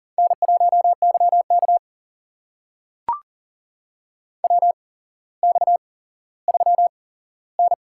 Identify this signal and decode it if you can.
Morse code, heard as a keyed tone, about 37 wpm, spaced as for 17 wpm, 700 Hz. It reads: N1YK WX3N
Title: USA Call Signs - 4 Characters - 8x Character Spacing - 40wpm